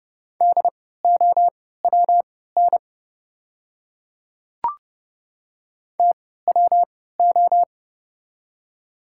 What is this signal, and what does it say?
Morse code, heard as a keyed tone, about 30 wpm, spaced as for 17 wpm, 690 Hz. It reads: DOWN TWO
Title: 3x Character Spacing - Top 100 Words - 30wpm